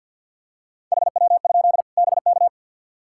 Morse code, heard as a keyed tone, about 51 wpm, 700 Hz. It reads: HW? BK
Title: K1USN SST QSO Practice 50wpm